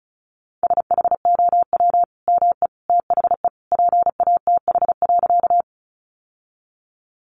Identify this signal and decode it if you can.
Morse code, heard as a keyed tone, about 34 wpm, 700 Hz. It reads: SHOW ME THE PATH.
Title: Sentences from Top 1400 Words 35wpm